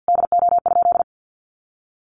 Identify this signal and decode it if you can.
Morse code, heard as a keyed tone, about 50 wpm, 700 Hz. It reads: DO?
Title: Sentences from Top 200 Words 50wpm